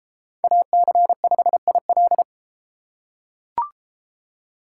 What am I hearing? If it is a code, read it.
Morse code, heard as a keyed tone, about 32 wpm, 700 Hz. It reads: AC5IL